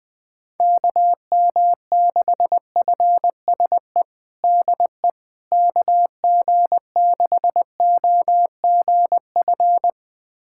Morse code, heard as a keyed tone, about 19 wpm, 700 Hz. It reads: KM6FSE DE KG6OGF
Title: USA Callsign DE Callsign - 6 Characters - 20wpm